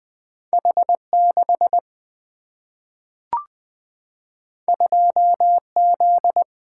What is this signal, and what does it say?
Morse code, heard as a keyed tone, about 20 wpm, 700 Hz. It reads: H6 2Z
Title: Single-Letter Single-Number 20wpm